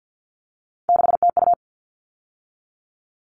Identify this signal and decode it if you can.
Morse code, heard as a keyed tone, about 50 wpm, 700 Hz. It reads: 6TU
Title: All Cut Numbers - 3-Digit 50wpm